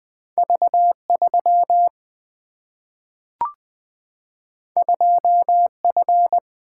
Morse code, heard as a keyed tone, about 20 wpm, 700 Hz. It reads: V3 2F
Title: Single-Letter Single-Number 20wpm